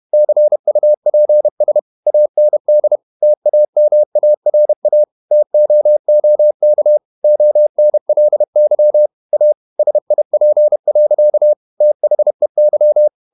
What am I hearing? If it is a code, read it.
Morse code, heard as a keyed tone, about 31 wpm, 600 Hz. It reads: CUPS AND TAMARA TOOK ONLY A SIP. THEY
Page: Never 31wpm | CW med Gnister